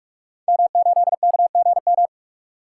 Morse code, heard as a keyed tone, about 44 wpm, 700 Hz. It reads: M8KGM
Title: Code Groups - 5 Letter:Numbers - 45wpm